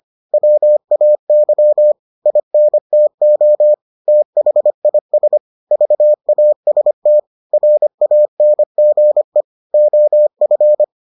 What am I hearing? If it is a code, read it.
Morse code, heard as a keyed tone, about 25 wpm, 600 Hz. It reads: WAY INTO THIS VAST RANGE OF